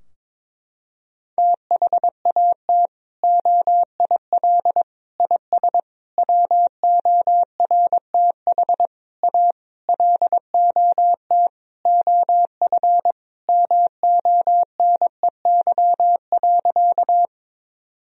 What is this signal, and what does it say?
Morse code, heard as a keyed tone, about 22 wpm, 700 Hz. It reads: THAT OIL IS WORTH A LOT OF MONEY.
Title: Sentences from Top 1000 Words - Review Part 4 - 22wpm